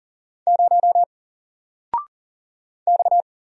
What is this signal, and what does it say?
Morse code, heard as a keyed tone, about 40 wpm, 700 Hz. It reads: 0 X
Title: Single Letter-Number Alt 40wpm